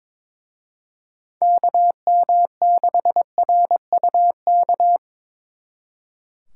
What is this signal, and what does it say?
Morse code, heard as a keyed tone, about 26 wpm, 700 Hz. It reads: KM6RUK
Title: Speed Racing - USA Call Signs 22wpm